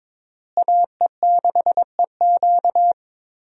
Morse code, heard as a keyed tone, about 22 wpm, 700 Hz. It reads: AE6EQ
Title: USA Call Signs - 5 Characters - Repeated 3x - 22wpm